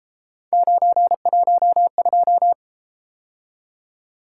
Morse code, heard as a keyed tone, about 32 wpm, 700 Hz. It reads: 912